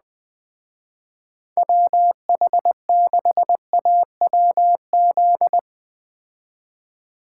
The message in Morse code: WH6AWZ

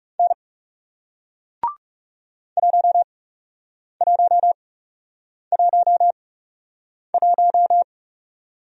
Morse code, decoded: N 1111

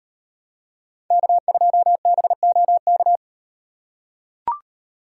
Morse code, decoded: K2BOK

